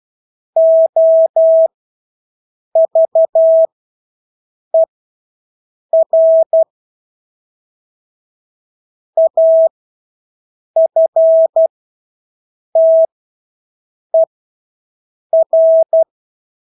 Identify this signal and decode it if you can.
Morse code, heard as a keyed tone, about 12 wpm, spaced as for 6 wpm, 650 Hz. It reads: OVER AFTER
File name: Engelske ord 12-6 wpm